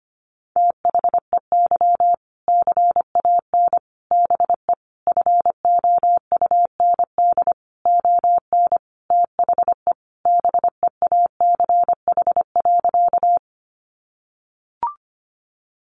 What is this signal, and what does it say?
Morse code, heard as a keyed tone, about 25 wpm, 700 Hz. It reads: THEY CAN BE FOUND ON THE BEACH.
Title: Sentences from Top 1500 Words 25wpm